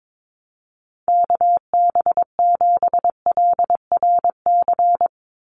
Morse code, heard as a keyed tone, about 22 wpm, 700 Hz. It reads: KB7LRC
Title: USA Call Signs 22wpm